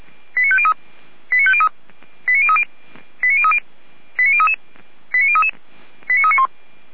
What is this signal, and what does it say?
Five-tone selective call (ZVEI), heard as quick runs of five tones, tones of about 70 ms. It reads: 89503 89503 89030 89030 89033 89033 89391
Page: SDR/ZVEI